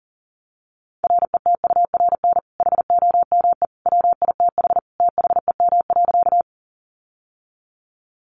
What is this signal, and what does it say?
Morse code, heard as a keyed tone, about 40 wpm, 700 Hz. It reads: RETURN HOME WITH THEM.
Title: Sentences from Top 800 Words 40wpm